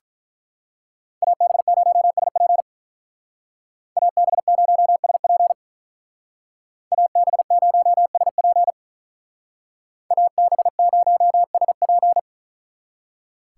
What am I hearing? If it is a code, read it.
Morse code, heard as a keyed tone, about 42 wpm, 700 Hz. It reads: AB0SP AB0SP AB0SP AB0SP
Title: Speed Racing - USA Call Signs 35wpm